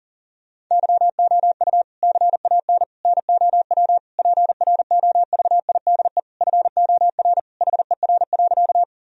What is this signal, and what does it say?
Morse code, heard as a keyed tone, about 40 wpm, 700 Hz. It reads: YOU CAN NOW PROVIDE FOR HER.
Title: Sentences from Top 1000 Words - Review Part 3 - 40wpm